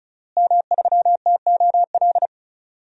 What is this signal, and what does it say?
Morse code, heard as a keyed tone, about 34 wpm, 700 Hz. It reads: M3TOL
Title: DXCC-100 Call Signs - 5 Characters - Repeated 3x - 35wpm